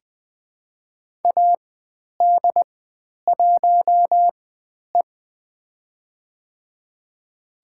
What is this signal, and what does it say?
Morse code, heard as a keyed tone, about 20 wpm, spaced as for 10 wpm, 700 Hz. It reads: AD1E
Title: 095 Lesson 16g - Letter c - Call Signs 20wpm